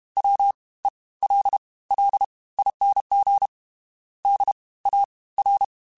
Morse code, heard as a keyed tone, about 32 wpm, spaced as for 20 wpm, 800 Hz. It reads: WELLING DAR
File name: Special INGs / Special INGs-mp3s / Fast ing @30